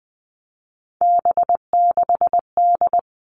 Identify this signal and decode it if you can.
Morse code, heard as a keyed tone, about 20 wpm, 700 Hz. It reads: B6D
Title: All Cut Numbers - 3-Digit 20wpm